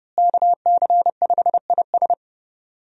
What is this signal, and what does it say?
Morse code, heard as a keyed tone, about 30 wpm, 700 Hz. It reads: KC5IS
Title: USA Call Signs - 5 Characters - Repeated 3x - 30wpm